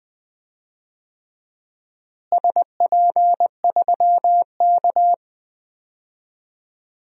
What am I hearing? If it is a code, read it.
Morse code, heard as a keyed tone, about 19 wpm, 700 Hz. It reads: SP3K
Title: DXCC-100 Call Signs - 4 Characters - Repeated 3x - 20wpm